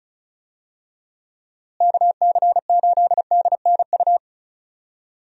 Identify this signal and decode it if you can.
Morse code, heard as a keyed tone, about 36 wpm, 700 Hz. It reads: KC8DNU
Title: USA Call Signs - 6 Characters - Repeated 3x - 35wpm